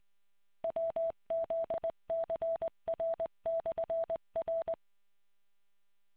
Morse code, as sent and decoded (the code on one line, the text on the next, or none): .-- --.. -.-. .-. -..-. .-.
WZCR/R